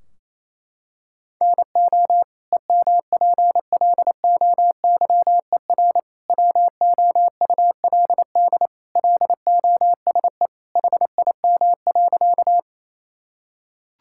Morse code, heard as a keyed tone, about 27 wpm, 700 Hz. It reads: NO EMPLOYER WOULD LOSE HIM.
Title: Sentences from Top 1600 Words 28wpm